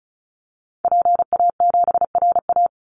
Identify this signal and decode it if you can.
Morse code, heard as a keyed tone, about 34 wpm, 700 Hz. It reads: PA7RA
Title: Call Signs Repeated 3x - 35wpm